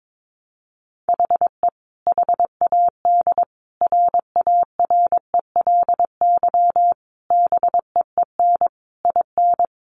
Morse code, heard as a keyed tone, about 22 wpm, 700 Hz. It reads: HE HAD RARELY BEEN IN
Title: Sentences from Top 1400 Words 22wpm